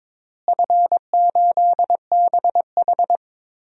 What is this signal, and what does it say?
Morse code, heard as a keyed tone, about 22 wpm, 700 Hz. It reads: F8BH
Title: DXCC-100 Call Signs - 4 Characters - Repeated 3x - 22wpm